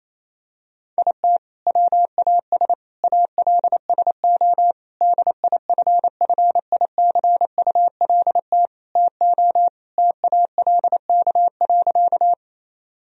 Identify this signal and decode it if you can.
Morse code, heard as a keyed tone, about 28 wpm, 700 Hz. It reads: IT WAS ALSO DIFFICULT TO TALK.
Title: Sentences from Top 1000 Words - Review Part 1 - 28wpm